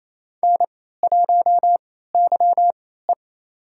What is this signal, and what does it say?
Morse code, heard as a keyed tone, about 28 wpm, spaced as for 16 wpm, 700 Hz. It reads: N1YE
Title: USA Call Signs - 4 Characters - 3x Character Spacing - 28wpm